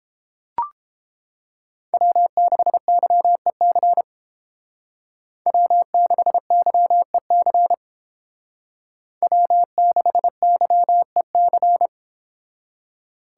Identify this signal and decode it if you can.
Morse code, heard as a keyed tone, about 30 wpm, 700 Hz. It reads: W6YEC W6YEC W6YEC